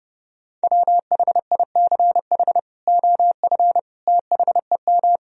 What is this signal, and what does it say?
Morse code, heard as a keyed tone, about 32 wpm, 700 Hz. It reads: WHICH OF THEM
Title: Sentences from Top 1000 Words - Review Part 3 - 30wpm